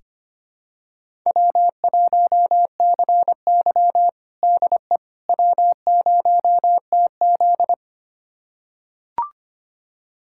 Morse code, heard as a keyed tone, about 25 wpm, 700 Hz. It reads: W1CY DE W0TZ
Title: USA Callsign DE Callsign - 4 Characters - 25wpm